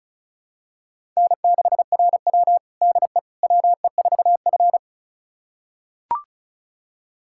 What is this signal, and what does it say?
Morse code, heard as a keyed tone, about 36 wpm, 700 Hz. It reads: N6RW DE WE4F
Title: USA Callsign DE Callsign - 4 Characters - 35wpm